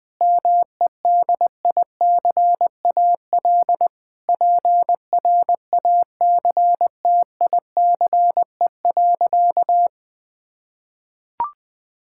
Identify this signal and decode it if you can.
Morse code, heard as a keyed tone, about 21 wpm, 700 Hz. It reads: MEDICAL PRACTICE.
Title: Sentences from Top 1000 Words - Review Part 5 - 20wpm